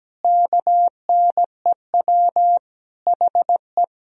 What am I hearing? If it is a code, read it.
Morse code, heard as a keyed tone, about 17 wpm, 700 Hz. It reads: KNEW HE
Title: Sentences from Top 1000 Words - Review Part 2 - 17wpm